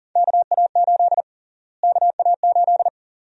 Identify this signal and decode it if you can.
Morse code, heard as a keyed tone, about 37 wpm, 700 Hz. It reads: KA8 KA8
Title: USA Call Sign Prefixes - Repeated 2x - 40wpm